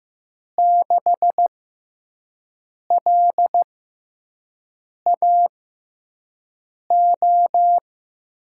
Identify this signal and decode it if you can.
Morse code, heard as a keyed tone, about 15 wpm, spaced as for 5 wpm, 700 Hz. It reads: 6LAO
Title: USA Call Signs - 6 Characters - 6x Character Spacing - 15wpm